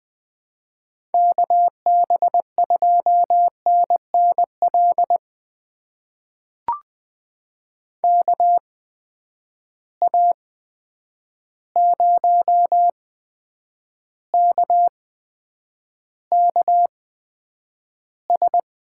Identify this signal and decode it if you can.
Morse code, heard as a keyed tone, about 21 wpm, spaced as for 9 wpm, 700 Hz. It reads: KB2NNL KA0KKS
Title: USA Call Signs - 6 Characters - 8x Character Spacing - 20wpm